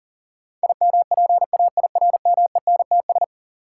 Morse code, heard as a keyed tone, about 40 wpm, 700 Hz. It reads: IMPAIRMENTS
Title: Sets of 5 Words Encore - New Word Drill - 40wpm